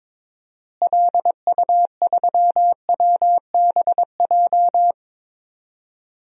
Morse code, heard as a keyed tone, about 22 wpm, 700 Hz. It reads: LU3WBJ
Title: DXCC-100 Call Signs - 6 Characters - Repeated 3x - 22wpm